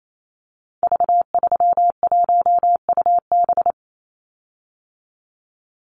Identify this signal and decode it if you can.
Morse code, heard as a keyed tone, about 28 wpm, 700 Hz. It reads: V31UB
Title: Call Signs Repeated 2x - 28wpm